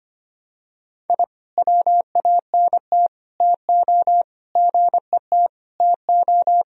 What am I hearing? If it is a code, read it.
Morse code, heard as a keyed tone, about 25 wpm, 700 Hz. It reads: I WANT TO GET TO
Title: Sentences from Top 1000 Words - Review Part 1 - 25wpm